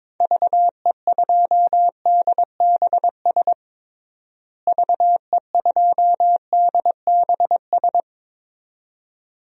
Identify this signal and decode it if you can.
Morse code, heard as a keyed tone, about 22 wpm, 700 Hz. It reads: VE2DBS VE2DBS
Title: DXCC-100 Call Signs - 6 Characters - Repeated 2x - 22wpm